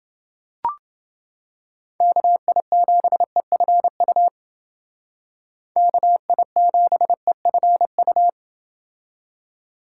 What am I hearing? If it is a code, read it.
Morse code, heard as a keyed tone, about 29 wpm, 700 Hz. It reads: KI7EFU KI7EFU